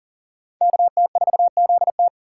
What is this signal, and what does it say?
Morse code, heard as a keyed tone, about 39 wpm, 690 Hz. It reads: KT4ZT
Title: USA Call Signs - 5 Characters - 6x Character Spacing - 40wpm